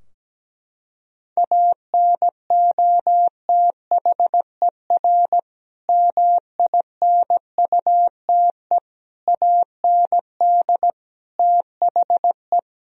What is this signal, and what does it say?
Morse code, heard as a keyed tone, about 17 wpm, 700 Hz. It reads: ANOTHER MINUTE AND THE
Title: Sentences from Top 1000 Words - Review Part 3 - 17wpm